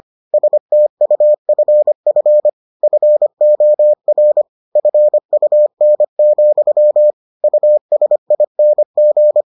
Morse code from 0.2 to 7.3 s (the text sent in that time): STUFF FOR FUN,